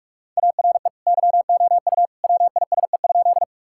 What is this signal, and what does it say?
Morse code, heard as a keyed tone, about 46 wpm, 700 Hz. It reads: ARE YOU WISE?
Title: Sentences from Top 1600 Words 45wpm